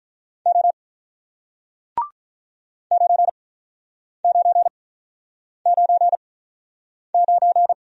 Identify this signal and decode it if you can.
Morse code, heard as a keyed tone, about 49 wpm, spaced as for 9 wpm, 700 Hz. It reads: O 9999